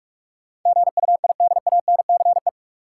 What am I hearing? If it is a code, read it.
Morse code, heard as a keyed tone, about 46 wpm, 700 Hz. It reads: GUIDANCE
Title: Sentences from Top 1600 Words - New Word Drill - 45wpm